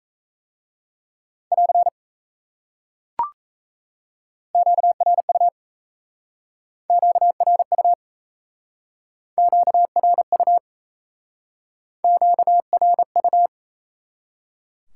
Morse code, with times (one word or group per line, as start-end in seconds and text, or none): 1.5-1.9 <AR>
4.5-5.5 QRU
6.9-7.9 QRU
9.4-10.6 QRU
12.0-13.5 QRU